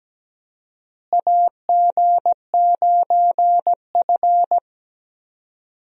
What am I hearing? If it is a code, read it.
Morse code, heard as a keyed tone, about 17 wpm, 700 Hz. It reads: AG9F
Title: USA Call Signs - 4 Characters - Repeated 3x - 17wpm